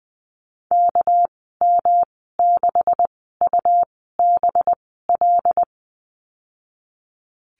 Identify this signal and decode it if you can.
Morse code, heard as a keyed tone, about 20 wpm, spaced as for 15 wpm, 700 Hz. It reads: KM6UBL
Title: 207 Lesson 33g - Letter j - Call Signs 20wpm